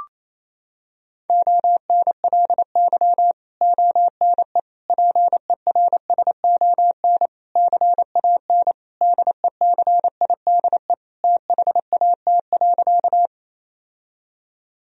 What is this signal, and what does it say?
Morse code, heard as a keyed tone, about 28 wpm, 700 Hz. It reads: ONLY ONE PERSON CAN DECIDE THAT.
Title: Sentences from Top 1000 Words - Review Part 3 - 28wpm